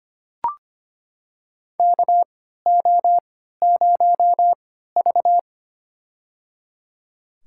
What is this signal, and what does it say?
Morse code, heard as a keyed tone, about 25 wpm, spaced as for 14 wpm, 700 Hz. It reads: KO0V
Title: USA Call Signs - 4 Characters - 3x Character Spacing - 25wpm